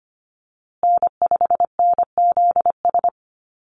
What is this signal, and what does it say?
Morse code, heard as a keyed tone, about 25 wpm, 700 Hz. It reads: N5NZS
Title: USA Call Signs Repeated 3x 25wpm